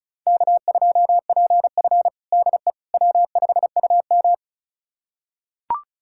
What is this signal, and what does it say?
Morse code, heard as a keyed tone, about 36 wpm, 700 Hz. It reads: K2PF DE W5UM
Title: USA Callsign DE Callsign - 4 Characters - 35wpm